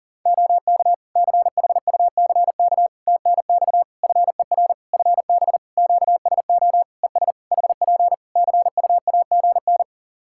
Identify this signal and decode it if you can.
Morse code, heard as a keyed tone, about 40 wpm, 700 Hz. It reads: OK CHUCK TNX FER FB QSO ES HP CUAGN
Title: Practice QSO - Exchange 4 - Ending - 40wpm